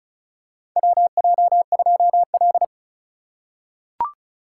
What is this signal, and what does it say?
Morse code, heard as a keyed tone, about 34 wpm, 700 Hz. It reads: WJ2L
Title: K1USN SST QSO Practice - 35wpm